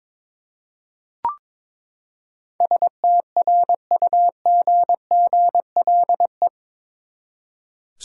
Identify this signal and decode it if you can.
Morse code, heard as a keyed tone, about 21 wpm, 700 Hz. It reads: STRUGGLE
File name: practice / 22wpm / Eight Letter Words - 22wpm